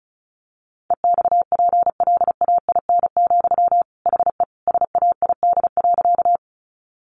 Morse code, heard as a keyed tone, about 34 wpm, 700 Hz. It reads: EXPLAIN, HE SAID.
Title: Sentences from Top 1100 Words 35wpm